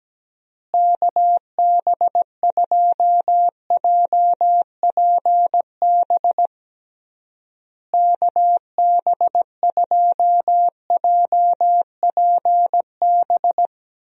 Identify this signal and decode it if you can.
Morse code, heard as a keyed tone, about 17 wpm, 700 Hz. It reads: KB2JPB KB2JPB
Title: USA Call Signs - 6 Characters - Repeated 3x - 17wpm